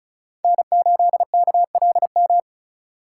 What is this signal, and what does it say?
Morse code, heard as a keyed tone, about 36 wpm, 700 Hz. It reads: N8KLM
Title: USA Call Signs - 5 Characters - 3x Character Spacing - 35wpm